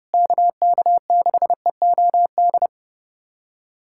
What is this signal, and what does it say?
Morse code, heard as a keyed tone, about 30 wpm, 700 Hz. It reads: KK6EOD
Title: USA Callsign DE Callsign 30wpm